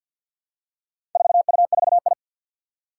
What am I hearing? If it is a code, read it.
Morse code, heard as a keyed tone, about 46 wpm, 700 Hz. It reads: 4U4I
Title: Four Challenging Letter-Number Combinations 50wpm